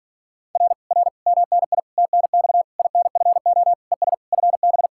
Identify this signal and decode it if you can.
Morse code, heard as a keyed tone, about 45 wpm, 700 Hz. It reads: R R MNI TNX INFO ES FB